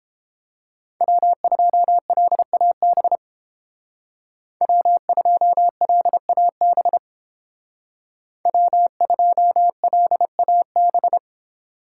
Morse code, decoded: W2LAB W2LAB W2LAB

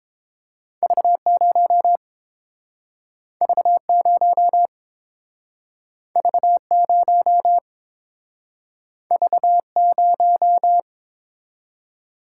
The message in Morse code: V0 V0 V0 V0

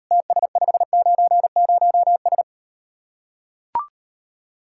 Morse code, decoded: TS590S

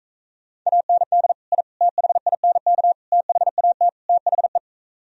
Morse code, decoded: AND I THINK THAT THE

